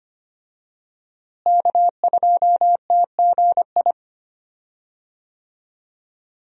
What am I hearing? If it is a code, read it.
Morse code, heard as a keyed tone, about 25 wpm, 700 Hz. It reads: K2TGI
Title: USA Call Signs - 5 Characters - Repeated 3x - 25wpm